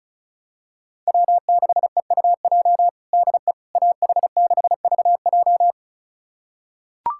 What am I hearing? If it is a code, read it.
Morse code, heard as a keyed tone, about 36 wpm, 700 Hz. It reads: W6EUJ DE AH6VJ
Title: USA Callsign DE Callsign - 5 Characters - 35wpm